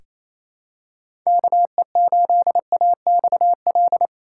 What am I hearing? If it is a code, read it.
Morse code, heard as a keyed tone, about 27 wpm, 700 Hz. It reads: KE8AXL
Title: USA Call Signs - 6 Characters - 4x Character Spacing - 28wpm